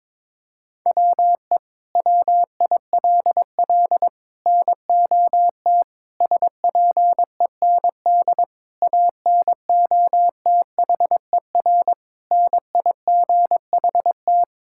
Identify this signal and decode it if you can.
Morse code, heard as a keyed tone, about 21 wpm, 700 Hz. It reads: WE WILL NOT SPEND ANOTHER NIGHT
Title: Sentences from Top 1000 Words - Review Part 5 - 22wpm